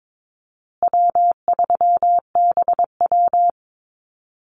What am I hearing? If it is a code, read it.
Morse code, heard as a keyed tone, about 22 wpm, 700 Hz. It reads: W3BW
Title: Easy Call Signs Repeated 3x - 22wpm